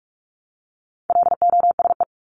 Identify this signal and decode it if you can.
Morse code, heard as a keyed tone, about 44 wpm, 700 Hz. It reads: LOSE
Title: Top 1001-1250 Words 45wpm